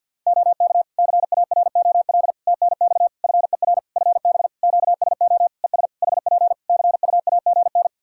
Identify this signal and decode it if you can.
Morse code, heard as a keyed tone, about 50 wpm, 700 Hz. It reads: OK CAROL TNX FER FB QSO ES HP CUAGN
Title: K1USN SST QSO Practice 50wpm